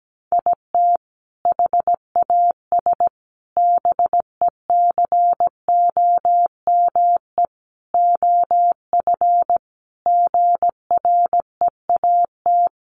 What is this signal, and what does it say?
Morse code, heard as a keyed tone, about 17 wpm, 700 Hz. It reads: IT HAS BECOME OF GREAT
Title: Sentences from Top 1200 Words 17wpm